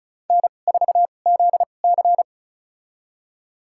Morse code, heard as a keyed tone, about 34 wpm, spaced as for 25 wpm, 700 Hz. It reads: N4ZC
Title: USA Call Signs - 4 Characters - 2x Character Spacing - 35wpm